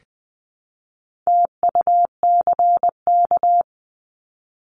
Morse code, heard as a keyed tone, about 20 wpm, 700 Hz. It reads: TUCK
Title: 260 Lesson 42e - Words 20-20wpm